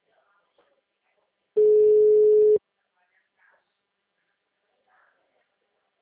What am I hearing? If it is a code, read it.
Ringback tone (Europe, 425 Hz), with speech.
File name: out- -2001- - - .12